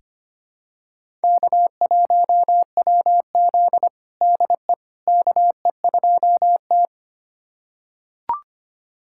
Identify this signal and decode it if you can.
Morse code, heard as a keyed tone, about 25 wpm, 700 Hz. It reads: K1WZ DE KE2T
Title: USA Callsign DE Callsign - 4 Characters - 25wpm